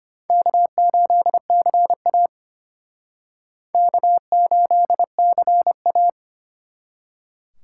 Morse code, heard as a keyed tone, about 31 wpm, 700 Hz. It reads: K8CA K8CA